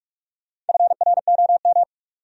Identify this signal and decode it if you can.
Morse code, heard as a keyed tone, about 60 wpm, spaced as for 52 wpm, 700 Hz. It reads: FROM
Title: Speed-Racing Top 500 Words 45wpm